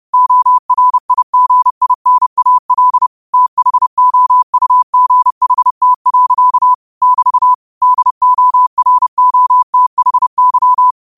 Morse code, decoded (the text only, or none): ORIGINAL THOUGHT. <BT> DOROTHY